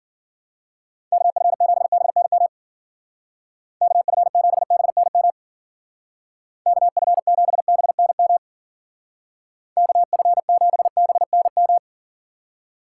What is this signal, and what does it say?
Morse code, heard as a keyed tone, about 50 wpm, 700 Hz. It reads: KF7BNM KF7BNM KF7BNM KF7BNM
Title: Speed Racing - USA 2x3 Call Signs 40wpm